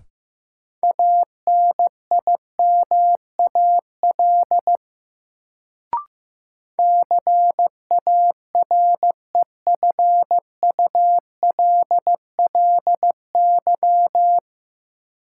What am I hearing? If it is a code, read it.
Morse code, heard as a keyed tone, about 15 wpm, 700 Hz. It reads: ANIMAL CAREFULLY
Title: Sentences from Top 1300 Words - New Word Drill Alt - 15wpm